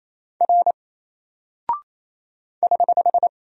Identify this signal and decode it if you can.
Morse code, heard as a keyed tone, about 27 wpm, 700 Hz. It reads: R <HH>
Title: Single Character (Letter:Number:Punctuation:Prosign) 28wpm